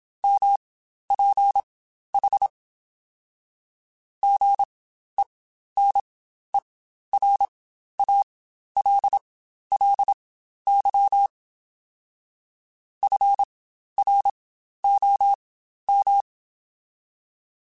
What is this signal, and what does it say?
Morse code, heard as a keyed tone, about 27 wpm, spaced as for 12 wpm, 800 Hz. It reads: MPH GENERALLY FROM
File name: Spaced Characters / Spaced Characters-mp3s / Weather 4 10 sp @25